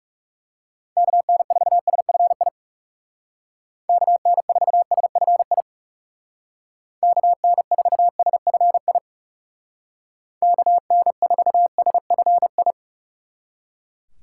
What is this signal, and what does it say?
Morse code, heard as a keyed tone, about 36 wpm, 700 Hz. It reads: KN4SFI KN4SFI KN4SFI KN4SFI
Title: Speed Racing - USA 2x3 Call Signs 30wpm